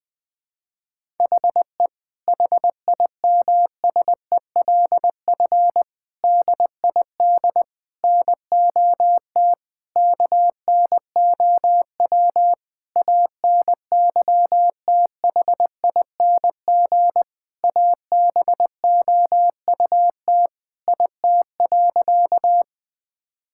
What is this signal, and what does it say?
Morse code, heard as a keyed tone, about 20 wpm, 700 Hz. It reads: HE HIMSELF DID NOT KNOW ANYTHING ABOUT IT.
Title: Sentences from Top 1600 Words 20wpm